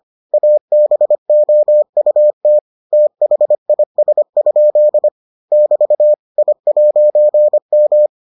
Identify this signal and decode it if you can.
Morse code, heard as a keyed tone, about 25 wpm, 600 Hz. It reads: ABOUT THIS? <BT> I'M